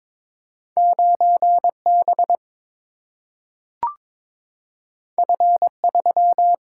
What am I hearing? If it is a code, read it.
Morse code, heard as a keyed tone, about 22 wpm, 700 Hz. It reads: 9B F3
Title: Single-Letter Single-Number 22wpm